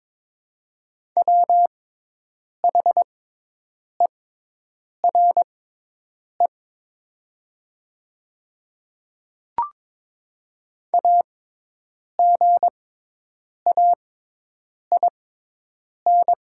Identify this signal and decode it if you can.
Morse code, heard as a keyed tone, about 23 wpm, spaced as for 8 wpm, 700 Hz. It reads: WHERE AGAIN
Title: 6x Character Spacing - Top 300 Words - 22wpm